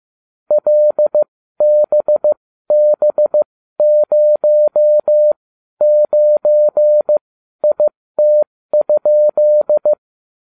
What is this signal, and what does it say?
Morse code, heard as a keyed tone, about 15 wpm, spaced as for 12 wpm, 600 Hz. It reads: LBB09IT?